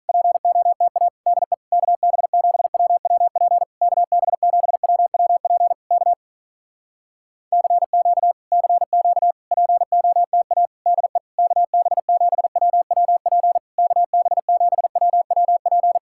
Morse code, heard as a keyed tone, about 47 wpm, 700 Hz. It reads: POTA DE KB7WWP KB7WWP K CQ CQ POTA DE KB7WWP KB7WWP